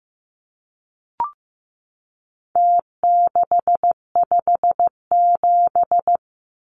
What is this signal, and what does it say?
Morse code, heard as a keyed tone, about 15 wpm, 700 Hz. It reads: T657
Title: Numbers - Contest Style - 4-Digit 15wpm